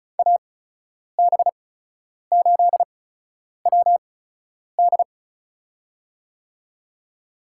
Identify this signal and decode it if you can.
Morse code, heard as a keyed tone, about 36 wpm, spaced as for 10 wpm, 700 Hz. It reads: AB8WD
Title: USA Call Signs - 8x Character Spacing - 35wpm